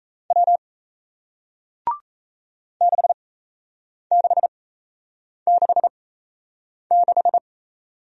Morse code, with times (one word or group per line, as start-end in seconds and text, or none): 0.3-0.6 W
2.8-7.4 6666